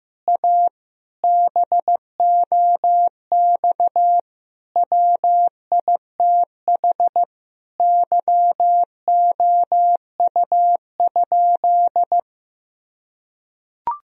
Morse code, heard as a keyed tone, about 16 wpm, 700 Hz. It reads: A BOX WITH YOU?
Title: Sentences from Top 1000 Words - Review Part 2 - 15wpm